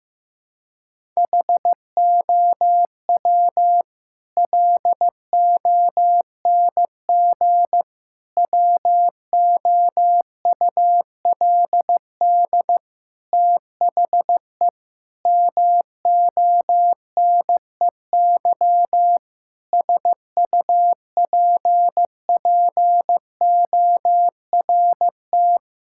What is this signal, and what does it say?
Morse code, heard as a keyed tone, about 16 wpm, 700 Hz. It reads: HOW LONG WOULD THE MONEY SUPPORT
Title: Sentences from Top 1000 Words - Review Part 5 - 15wpm